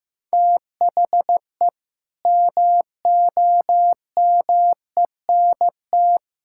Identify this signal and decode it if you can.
Morse code, heard as a keyed tone, about 15 wpm, 700 Hz. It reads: THE MOMENT
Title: Sentences from Top 1000 Words - Review Part 5 - 15wpm